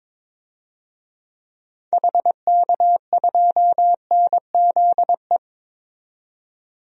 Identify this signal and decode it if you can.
Morse code, heard as a keyed tone, about 21 wpm, 700 Hz. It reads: HK2NZE
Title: DXCC-100 Call Signs - 6 Characters - 22wpm